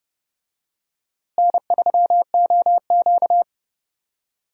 Morse code, heard as a keyed tone, about 30 wpm, 700 Hz. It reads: N3OQ
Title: USA Call Signs - 4 Characters - Repeated 2x - 30wpm